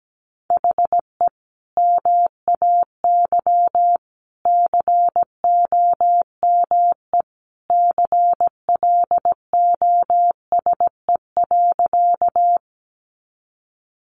Morse code, heard as a keyed tone, about 17 wpm, 700 Hz. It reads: HE MAY COME CLOSE.
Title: Sentences from Top 300 Words 17wpm